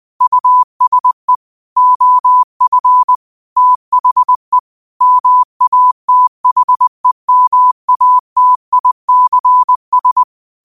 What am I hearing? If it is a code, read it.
Morse code, heard as a keyed tone, about 20 wpm, 1000 Hz. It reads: USE OF THE MATHEMATICS
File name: qotdfiles / 20wpm